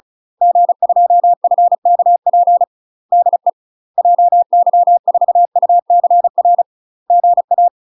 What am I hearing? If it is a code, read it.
Morse code, heard as a keyed tone, about 36 wpm, 700 Hz. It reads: G2FKP DE JY4UCR GA
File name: UK MASTER QSOs TRAINING / 35WPM / UK MASTER QSOs 0022